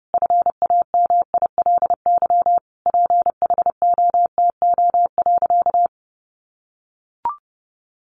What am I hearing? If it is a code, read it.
Morse code, heard as a keyed tone, about 30 wpm, 700 Hz. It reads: FAMILY PHOTO.
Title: Sentences from Top 1200 Words 30wpm